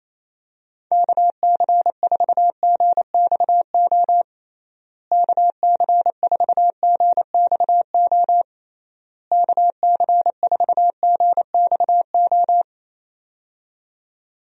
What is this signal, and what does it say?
Morse code, heard as a keyed tone, about 27 wpm, 700 Hz. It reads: KC4GXO KC4GXO KC4GXO
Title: USA Call Signs - 6 Characters - Repeated 3x - 28wpm